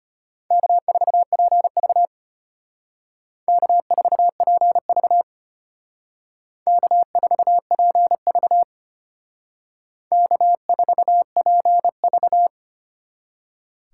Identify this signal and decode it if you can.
Morse code, heard as a keyed tone, about 29 wpm, 700 Hz. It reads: K4PV K4PV K4PV K4PV
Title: Speed Racing - USA 1x2 Call Signs 25wpm